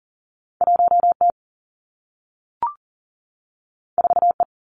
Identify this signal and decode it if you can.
Morse code, heard as a keyed tone, about 42 wpm, 700 Hz. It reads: 1 T 4 E
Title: Two Challenging Letter-Number Combinations 40wpm